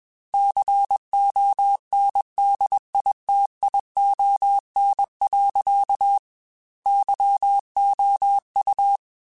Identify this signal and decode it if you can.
Morse code, heard as a keyed tone, about 21 wpm, 800 Hz. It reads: CONDITION. YOU
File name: Technical Subjects / Technical Subjects-mp3s / OR Gate @20